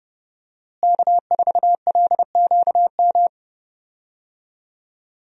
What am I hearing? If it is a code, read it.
Morse code, heard as a keyed tone, about 29 wpm, 700 Hz. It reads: K4LQM
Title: USA Call Signs - 5 Characters - Repeated 3x - 30wpm